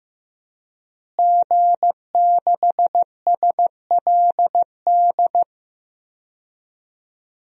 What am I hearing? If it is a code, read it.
Morse code, heard as a keyed tone, about 15 wpm, 700 Hz. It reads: G6SLD
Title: DXCC-100 Call Signs - 5 Characters - Repeated 3x - 15wpm